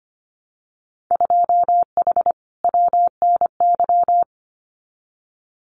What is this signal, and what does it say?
Morse code, heard as a keyed tone, about 25 wpm, 700 Hz. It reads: 2H WNY
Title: Winter Field Day Exchanges Encore 25wpm